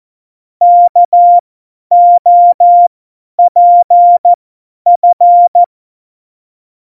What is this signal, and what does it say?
Morse code, heard as a keyed tone, about 14 wpm, spaced as for 10 wpm, 700 Hz. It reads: KOPF